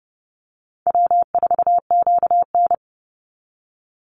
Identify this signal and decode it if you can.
Morse code, heard as a keyed tone, about 30 wpm, 700 Hz. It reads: W4QN
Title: Call Signs - 30wpm